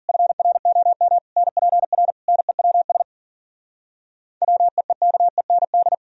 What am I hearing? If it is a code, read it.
Morse code, heard as a keyed tone, about 52 wpm, 700 Hz. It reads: FROM NPR NEWS WEEKEND